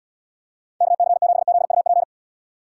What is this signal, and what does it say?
Morse code, heard as a keyed tone, about 78 wpm, 700 Hz. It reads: KC7ZUP